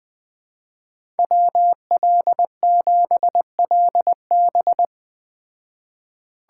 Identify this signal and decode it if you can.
Morse code, heard as a keyed tone, about 20 wpm, 700 Hz. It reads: WL7LB
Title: Speed Racing - USA 2x2 Call Signs 17wpm